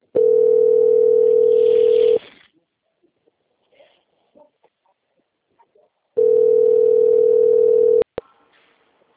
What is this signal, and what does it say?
Ringback tone (North America), 440 + 480 Hz.
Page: Cosmopolis - SP